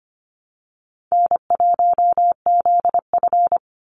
Morse code, heard as a keyed tone, about 25 wpm, 700 Hz. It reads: N1ZF
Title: NA SOTA Activator Callsigns 25wpm